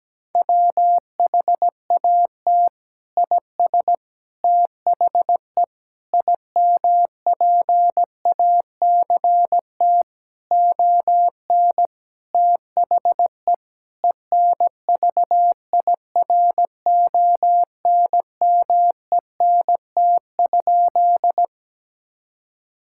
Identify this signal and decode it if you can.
Morse code, heard as a keyed tone, about 17 wpm, 700 Hz. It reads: WHAT IS THE IMPACT ON THE ENVIRONMENT?
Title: Sentences from Top 1000 Words - Review Part 6 - 17wpm